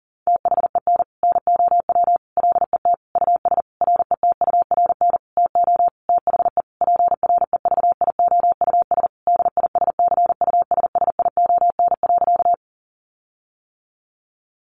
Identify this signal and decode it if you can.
Morse code, heard as a keyed tone, about 40 wpm, 700 Hz. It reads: THEN NOW LET US RETURN TO THE PREVIOUS DISCUSSION.
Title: Sentences from Top 1500 Words 40wpm